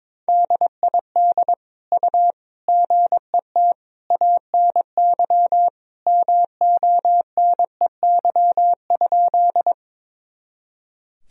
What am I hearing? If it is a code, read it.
Morse code, heard as a keyed tone, about 22 wpm, 700 Hz. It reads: DID U GET ANY MONEY?
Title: Sentences from Top 400 Words and Abbreviations 22wpm